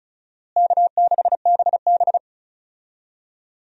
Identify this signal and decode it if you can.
Morse code, heard as a keyed tone, about 34 wpm, 700 Hz. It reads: K6BB
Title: USA Call Signs - 4 Characters - Repeated 3x - 35wpm